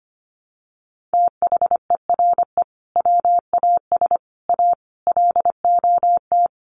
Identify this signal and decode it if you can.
Morse code, heard as a keyed tone, about 25 wpm, 700 Hz. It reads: THERE WAS A LOT
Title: Sets of 5 Words 25wpm